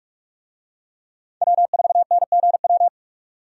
Morse code, heard as a keyed tone, about 47 wpm, 700 Hz. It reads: W4NGW
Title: USA Call Signs - 5 Characters - 45wpm